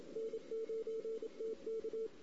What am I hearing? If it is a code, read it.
Morse code, heard as a keyed tone, about 25 wpm, 450 Hz. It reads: N9TK
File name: CW / 2008-ss-cw-k5tr-op-n5rz -16:00